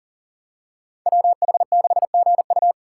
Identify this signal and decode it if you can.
Morse code, heard as a keyed tone, about 39 wpm, 700 Hz. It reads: WH6GU
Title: USA Call Signs - 5 Characters - 6x Character Spacing - 40wpm